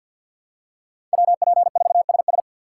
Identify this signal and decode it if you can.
Morse code, heard as a keyed tone, about 52 wpm, 700 Hz. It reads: WP4SS
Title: USA Call Signs - 5 Characters - 2x Character Spacing - 50wpm